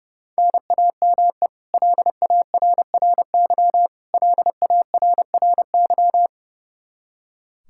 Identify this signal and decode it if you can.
Morse code, heard as a keyed tone, about 30 wpm, 700 Hz. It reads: NAME LARRY LARRY
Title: Practice QSO - Exchange 1 - Beginning - 30wpm